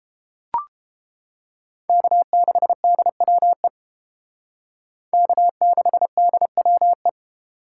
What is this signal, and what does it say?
Morse code, heard as a keyed tone, about 30 wpm, 700 Hz. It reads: K6DWE K6DWE